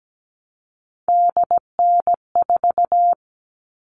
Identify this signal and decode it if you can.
Morse code, heard as a keyed tone, about 17 wpm, 700 Hz. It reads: DN4
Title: All Cut Numbers - 3-Digit 17wpm